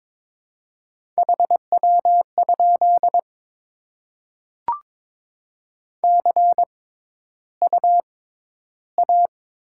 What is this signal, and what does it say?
Morse code, heard as a keyed tone, about 22 wpm, 690 Hz. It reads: H W ? C U A
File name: Top 20 QSO Elements - 6x Character Spacing - 22wpm